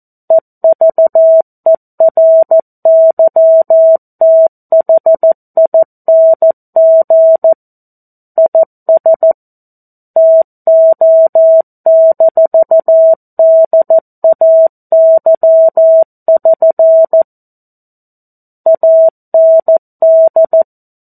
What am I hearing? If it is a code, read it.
Morse code, heard as a keyed tone, about 14 wpm, 650 Hz. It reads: EVERYTHING IS TO-DAY<SN> AND
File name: 02-Chapter-02 14 WPM